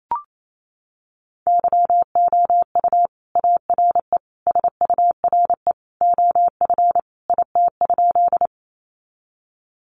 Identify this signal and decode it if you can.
Morse code, heard as a keyed tone, about 28 wpm, 700 Hz. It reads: YOU ARE SURE OF IT?
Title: Sentences from Top 300 Words 28wpm